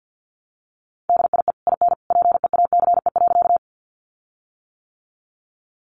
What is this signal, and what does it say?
Morse code, heard as a keyed tone, about 48 wpm, 700 Hz. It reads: DIE IN PEACE.
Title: Sentences from Top 1500 Words 50wpm